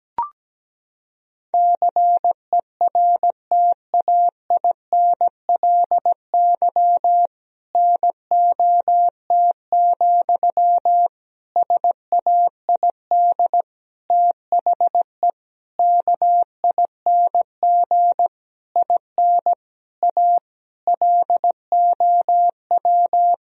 Morse code, heard as a keyed tone, about 17 wpm, 700 Hz. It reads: CERTAINLY NOT, SAID THE KING IN A LOW
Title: Sentences from Top 1000 Words - Review Part 1 - 17wpm